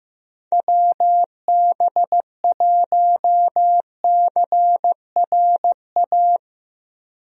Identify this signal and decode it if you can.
Morse code, heard as a keyed tone, about 15 wpm, 700 Hz. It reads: WB1CRA
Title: USA Call Signs - 6 Characters - 15wpm